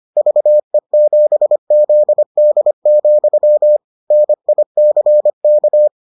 Morse code, heard as a keyed tone, about 25 wpm, 600 Hz. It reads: VE7ZD, NICK